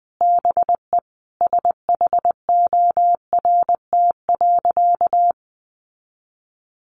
Morse code, heard as a keyed tone, about 20 wpm, 700 Hz. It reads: BE SHORT.
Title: Sentences from Top 1400 Words 20wpm